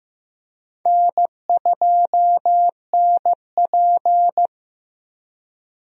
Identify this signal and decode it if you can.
Morse code, heard as a keyed tone, about 15 wpm, 700 Hz. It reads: N2NP
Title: USA Call Signs - 4 Characters - 15wpm